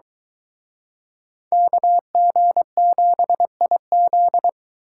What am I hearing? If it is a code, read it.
Morse code, heard as a keyed tone, about 23 wpm, 700 Hz. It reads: KG7IZ